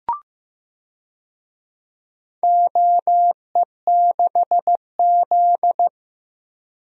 Morse code, heard as a keyed tone, about 15 wpm, 700 Hz. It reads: OE6Z
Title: DXCC-100 Call Signs - 4 Characters - Repeated 2x - 15wpm